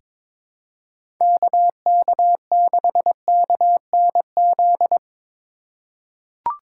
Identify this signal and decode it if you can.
Morse code, heard as a keyed tone, about 22 wpm, 700 Hz. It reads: KK6KNZ
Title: USA Call Signs - 4x Character Spacing - 22wpm